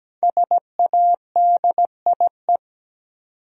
Morse code, heard as a keyed tone, about 17 wpm, 700 Hz. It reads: SADIE
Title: USA Common Names - 17wpm